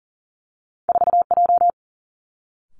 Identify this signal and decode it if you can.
Morse code, heard as a keyed tone, about 42 wpm, 700 Hz. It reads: 4J
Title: Two Challenging Letter-Number Combinations 40wpm